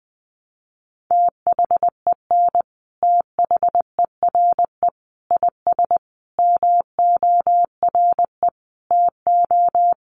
Morse code, heard as a keyed tone, about 20 wpm, 700 Hz. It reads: THEN THERE IS MORE TO
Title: Sentences from Top 100 Words 20wpm